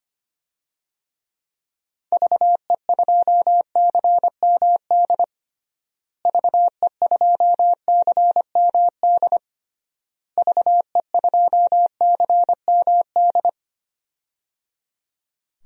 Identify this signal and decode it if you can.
Morse code, heard as a keyed tone, about 25 wpm, 700 Hz. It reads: VE2CMD VE2CMD VE2CMD
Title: DXCC-100 Call Signs - 6 Characters - Repeated 3x - 25wpm